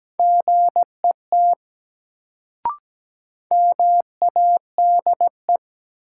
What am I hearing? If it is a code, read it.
Morse code, heard as a keyed tone, about 17 wpm, 700 Hz. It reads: GET MADE
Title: Sentences from Top 200 Words - New Word Drill - 17wpm